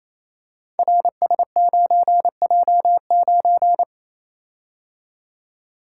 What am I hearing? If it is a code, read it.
Morse code, heard as a keyed tone, about 27 wpm, 700 Hz. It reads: RS9J9
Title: Code Groups - 5 Letter:Numbers - 28wpm